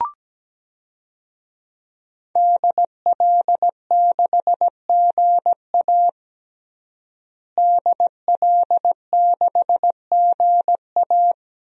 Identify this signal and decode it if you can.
Morse code, heard as a keyed tone, about 17 wpm, 700 Hz. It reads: DL6GA DL6GA
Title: DXCC-100 Call Signs - 5 Characters - Repeated 2x - 17wpm